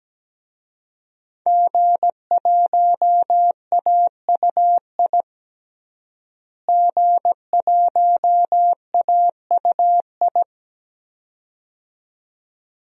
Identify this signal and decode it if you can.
Morse code, heard as a keyed tone, about 17 wpm, 700 Hz. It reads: G1AUI G1AUI
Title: DXCC-100 Call Signs - 5 Characters - Repeated 2x - 17wpm